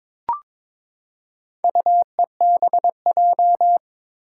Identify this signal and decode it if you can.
Morse code, heard as a keyed tone, about 22 wpm, 700 Hz. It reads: UEBJ
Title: Four Challenging Letter-Number Combinations 22wpm